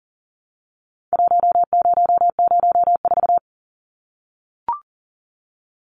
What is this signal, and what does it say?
Morse code, heard as a keyed tone, about 40 wpm, 700 Hz. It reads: 1004
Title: Numbers - Contest Style - 4-Digit 40wpm